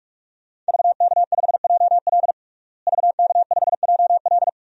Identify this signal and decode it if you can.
Morse code, heard as a keyed tone, about 47 wpm, 700 Hz. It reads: VK5JL VK5JL
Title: DXCC-100 Call Signs - 5 Characters - Repeated 3x - 45wpm